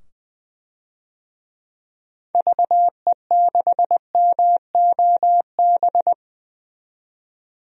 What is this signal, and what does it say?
Morse code, heard as a keyed tone, about 20 wpm, 700 Hz. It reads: VE6MOB
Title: DXCC-100 Call Signs - 6 Characters - 20wpm